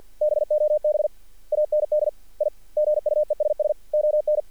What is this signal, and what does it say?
Morse code, heard as a keyed tone, about 50 wpm, 600 Hz. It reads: BOB AND I GUESS ON
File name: archived / CRSnet